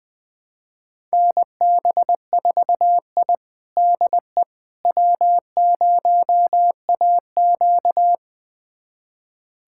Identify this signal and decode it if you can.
Morse code, heard as a keyed tone, about 20 wpm, 700 Hz. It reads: NB4I DE W0AQ
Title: USA Callsign DE Callsign - 4 Characters - 20wpm